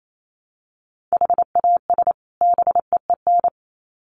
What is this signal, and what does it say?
Morse code, heard as a keyed tone, about 28 wpm, 700 Hz. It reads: HAS BEEN
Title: Sets of 2 Words 28wpm